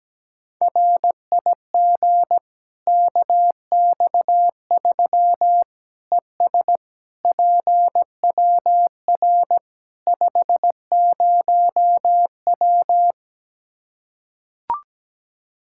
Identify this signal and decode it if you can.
Morse code, heard as a keyed tone, about 17 wpm, 700 Hz. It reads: RIG KX3 ES PWR 50W
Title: Practice QSO - Exchange 2 - Info - 17wpm